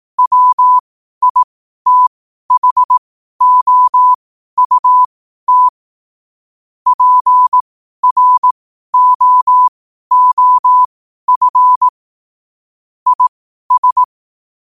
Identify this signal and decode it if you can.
Morse code, heard as a keyed tone, about 18 wpm, spaced as for 12 wpm, 1000 Hz. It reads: WITHOUT PROOF IS
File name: qotdfiles / 12wpm